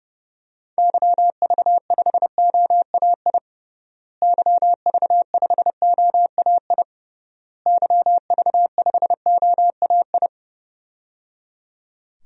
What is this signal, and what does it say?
Morse code, heard as a keyed tone, about 30 wpm, 700 Hz. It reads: YV5OAI YV5OAI YV5OAI
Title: DXCC-100 Call Signs - 6 Characters - Repeated 3x - 30wpm